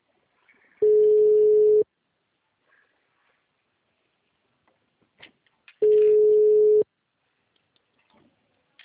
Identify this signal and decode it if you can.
Ringback tone (Europe, 425 Hz).
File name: out- -2002- - - .45